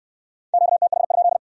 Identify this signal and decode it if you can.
Morse code, heard as a keyed tone, about 61 wpm, 700 Hz. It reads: QTH?